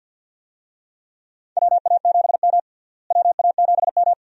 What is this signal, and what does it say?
Morse code, heard as a keyed tone, about 47 wpm, 700 Hz. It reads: WA7M WA7M
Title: USA Call Signs - 4 Characters - Repeated 2x - 50wpm